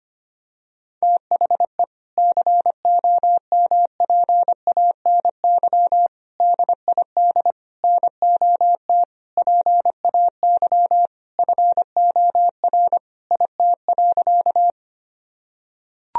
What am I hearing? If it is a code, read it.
Morse code, heard as a keyed tone, about 25 wpm, 700 Hz. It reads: THE COMPANY DID NOT PAY FOR IT.
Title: Sentences from Top 1000 Words - Review Part 3 - 25wpm